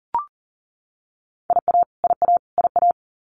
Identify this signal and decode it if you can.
Morse code, heard as a keyed tone, about 40 wpm, 700 Hz. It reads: IA IA IA
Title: US States Repeated 3x 40wpm